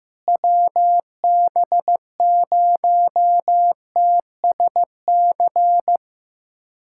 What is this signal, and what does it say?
Morse code, heard as a keyed tone, about 15 wpm, 700 Hz. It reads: WB0TSC
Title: USA Callsign DE Callsign 15wpm